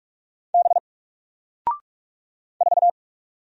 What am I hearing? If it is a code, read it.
Morse code, heard as a keyed tone, about 46 wpm, 700 Hz. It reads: B 4
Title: Single Letter-Number Alt 45wpm